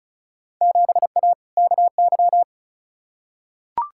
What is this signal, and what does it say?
Morse code, heard as a keyed tone, about 34 wpm, 700 Hz. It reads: 7A KY
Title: Field Day Exchanges Encore - 35wpm